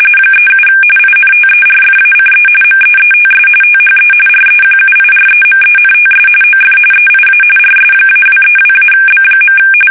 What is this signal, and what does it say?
RTTY, about 45.45 baud. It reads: 37 INTERFACE MEETS THE REQUIREMENTS OF EIA RS-232-B.
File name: fsk850 / teletype-corp